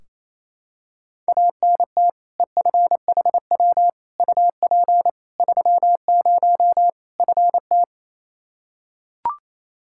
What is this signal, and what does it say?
Morse code, heard as a keyed tone, about 28 wpm, 700 Hz. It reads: ANT EFHW UP 30 FT
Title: Practice QSO - Exchange 2 - Info - 28wpm